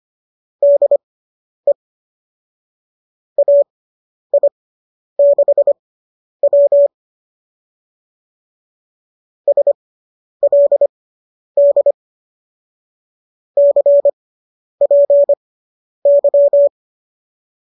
Morse code, heard as a keyed tone, about 25 wpm, spaced as for 10 wpm, 590 Hz. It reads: DE AI6W SLD CPY